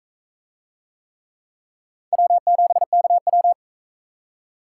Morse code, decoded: W7KW

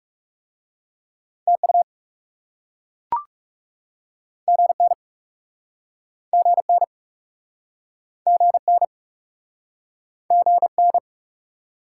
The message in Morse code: TU GN GN GN GN

